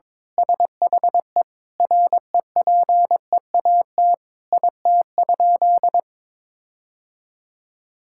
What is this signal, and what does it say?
Morse code, heard as a keyed tone, about 22 wpm, 700 Hz. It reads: SHE REPEAT IT?
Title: Sentences from Top 1600 Words 22wpm